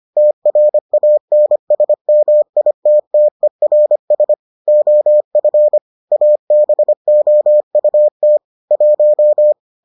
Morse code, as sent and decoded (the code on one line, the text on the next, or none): - .-. .- -. ... -- .. - - . .-. ... / --- ..-. / .- -... --- ..- - / .----
TRANSMITTERS OF ABOUT 1